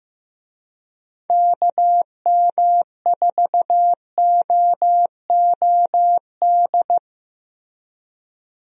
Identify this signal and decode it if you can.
Morse code, heard as a keyed tone, about 15 wpm, 700 Hz. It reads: KM4OOD
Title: USA Call Signs - 6 Characters - Repeated 3x - 15wpm